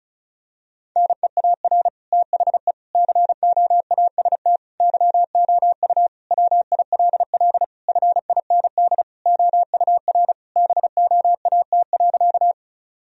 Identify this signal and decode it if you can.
Morse code, heard as a keyed tone, about 36 wpm, 700 Hz. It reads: NEAR THE COAST YOU WILL FIND OUR BOAT.
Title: Sentences from Top 1000 Words - Review Part 4 - 35wpm